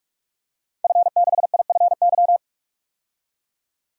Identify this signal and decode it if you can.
Morse code, heard as a keyed tone, about 42 wpm, 700 Hz. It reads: F6IFY
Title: DXCC-100 Call Signs - 5 Characters - 45wpm